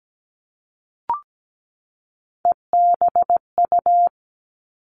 Morse code, heard as a keyed tone, about 17 wpm, 700 Hz. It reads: EBU
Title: All Cut Numbers - 3-Digit 17wpm